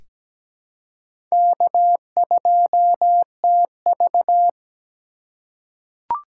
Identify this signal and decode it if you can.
Morse code, heard as a keyed tone, about 17 wpm, 700 Hz. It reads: K2TV
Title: K1USN SST QSO Practice - 17wpm